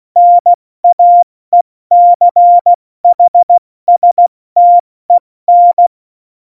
Morse code, heard as a keyed tone, about 16 wpm, spaced as for 14 wpm, 700 Hz. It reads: NAECHSTEN
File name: Morsekurs / Witze / Witz 23. Oktober 2018, 70 F80